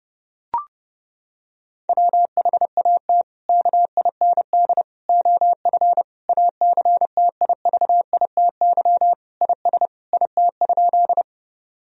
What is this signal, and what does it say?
Morse code, heard as a keyed tone, about 30 wpm, 700 Hz. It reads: WHAT KIND OF ACTIVITY IS IT?
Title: Sentences from Top 1000 Words - Review Part 4 - 30wpm